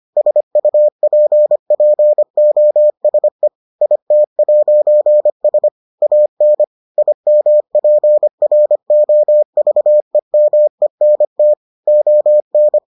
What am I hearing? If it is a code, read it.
Morse code, heard as a keyed tone, about 25 wpm, 600 Hz. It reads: SUPPOSE IT'S AN IMPROVEMENT ON